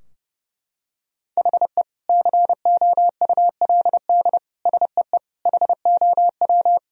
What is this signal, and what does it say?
Morse code, heard as a keyed tone, about 30 wpm, 700 Hz. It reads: HE COULD SEE HOW
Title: Sentences from Top 1000 Words - Review Part 3 - 30wpm